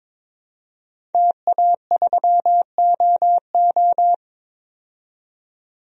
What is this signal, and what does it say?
Morse code, heard as a keyed tone, about 22 wpm, 700 Hz. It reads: TA3OO
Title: DXCC-100 Call Signs - 5 Characters - 22wpm